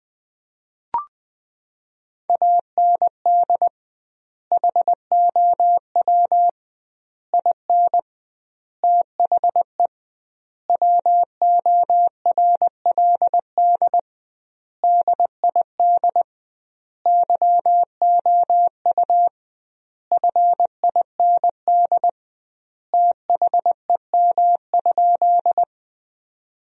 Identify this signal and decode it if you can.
Morse code, heard as a keyed tone, about 21 wpm, 700 Hz. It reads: AND HOW IN THE WORLD DID YOU FIND THEM?
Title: 2x Word-Spacing - Sentences from Top 200 Words - 20wpm